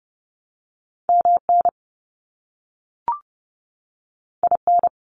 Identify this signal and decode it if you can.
Morse code, heard as a keyed tone, about 30 wpm, 700 Hz. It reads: MN IN
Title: US States 30wpm